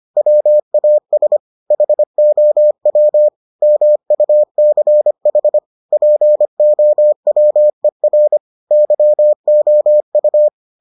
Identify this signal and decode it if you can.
Morse code, heard as a keyed tone, about 25 wpm, 600 Hz. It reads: WAS HOW MUCH POWER YOU